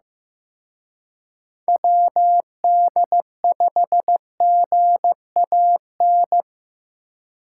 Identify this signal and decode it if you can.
Morse code, heard as a keyed tone, about 18 wpm, 700 Hz. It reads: WD5GAN
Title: Speed Racing - USA 2x3 Call Signs 15wpm